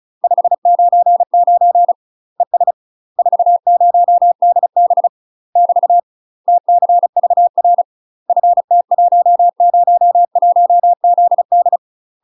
Morse code, heard as a keyed tone, about 35 wpm, 700 Hz. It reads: 599 ES 40DB <BT> TCVR FT101ZD
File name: UK MASTER QSOs TRAINING / 35WPM / UK MASTER QSOs 0023